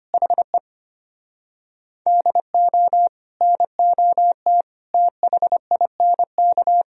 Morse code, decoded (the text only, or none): HE DO NOT THINK